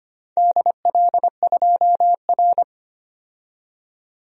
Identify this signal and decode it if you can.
Morse code, heard as a keyed tone, about 25 wpm, 700 Hz. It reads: DL2R
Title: DXCC-100 Call Signs - 4 Characters - 25wpm